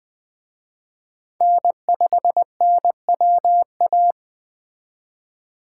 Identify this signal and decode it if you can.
Morse code, heard as a keyed tone, about 21 wpm, 700 Hz. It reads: N5NWA